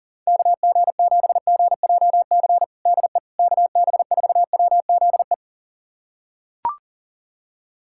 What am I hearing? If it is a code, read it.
Morse code, heard as a keyed tone, about 40 wpm, 700 Hz. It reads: KG7GJC DE KB4WZE
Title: USA Callsign DE Callsign 40wpm